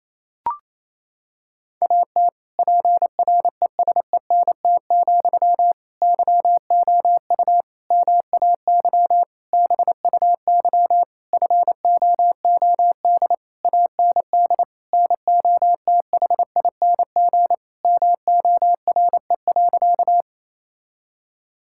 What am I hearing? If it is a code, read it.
Morse code, heard as a keyed tone, about 27 wpm, 700 Hz. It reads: AT PRESENT, YOU MAY BUY FOOD AND NOTHING MORE.
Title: Sentences from Top 1000 Words - Review Part 5 - 28wpm